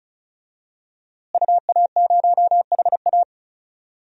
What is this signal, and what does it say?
Morse code, heard as a keyed tone, about 34 wpm, 700 Hz. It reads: UA0HA
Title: DXCC-100 Call Signs - 5 Characters - Repeated 3x - 35wpm